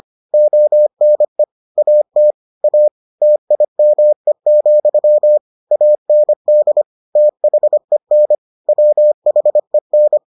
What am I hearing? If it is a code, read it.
Morse code, heard as a keyed tone, about 25 wpm, 600 Hz. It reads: ONE AT A TIME, AND THEN WHEN